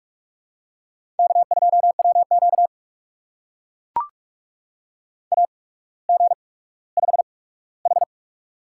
Morse code, heard as a keyed tone, about 44 wpm, spaced as for 19 wpm, 700 Hz. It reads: K2WQ AG5H
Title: USA Call Signs - 4 Characters - 8x Character Spacing - 45wpm